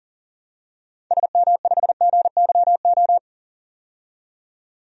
Morse code, heard as a keyed tone, about 40 wpm, 700 Hz. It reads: SM5GYO
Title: DXCC-100 Call Signs - 6 Characters - 40wpm